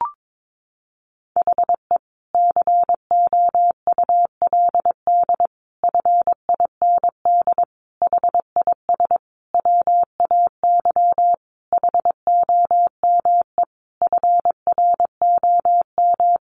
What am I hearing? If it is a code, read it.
Morse code, heard as a keyed tone, about 22 wpm, 700 Hz. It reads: HE COULD FIND HIS WAY HOME FROM
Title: Sentences from Top 1400 Words 22wpm